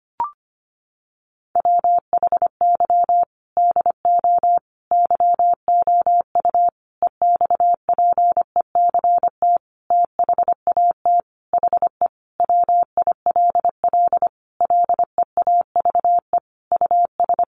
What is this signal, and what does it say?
Morse code, heard as a keyed tone, about 25 wpm, 690 Hz. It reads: WHY DO YOU EXPECT THAT HE WILL LEAVE US
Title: Sentences from Top 1100 Words 25wpm